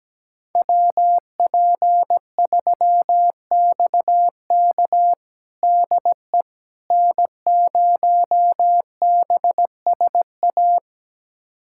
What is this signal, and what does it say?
Morse code, heard as a keyed tone, about 17 wpm, 700 Hz. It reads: WP3XK DE N0BSA
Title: USA Callsign DE Callsign - 5 Characters - 17wpm